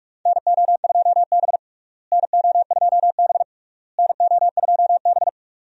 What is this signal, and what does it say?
Morse code, heard as a keyed tone, about 41 wpm, 700 Hz. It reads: NO2B NO2B NO2B
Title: USA Call Signs - 4 Characters - Repeated 3x - 45wpm